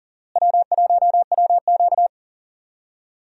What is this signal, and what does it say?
Morse code, heard as a keyed tone, about 40 wpm, 700 Hz. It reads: W1WQ
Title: USA Call Signs - 4 Characters - Repeated 2x - 40wpm